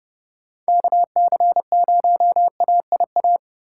Morse code, heard as a keyed tone, about 32 wpm, 700 Hz. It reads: KC0AIA
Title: USA Call Signs - 2x Character Spacing - 30wpm